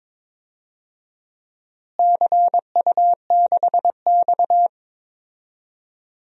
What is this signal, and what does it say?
Morse code, heard as a keyed tone, about 21 wpm, 700 Hz. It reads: CU6X
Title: DXCC-100 Call Signs - 4 Characters - Repeated 2x - 22wpm